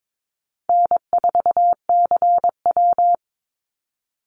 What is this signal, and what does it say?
Morse code, heard as a keyed tone, about 22 wpm, 700 Hz. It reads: N4CW
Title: Call Signs - 22wpm